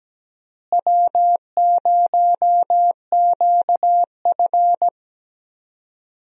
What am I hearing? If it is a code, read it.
Morse code, heard as a keyed tone, about 17 wpm, 700 Hz. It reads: W0QF
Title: USA Call Signs - 4 Characters - Repeated 2x - 17wpm